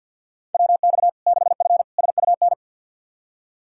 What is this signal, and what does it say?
Morse code, heard as a keyed tone, about 52 wpm, 700 Hz. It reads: WX 6F SUN
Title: Practice QSO - Exchange 2 - Info - 50wpm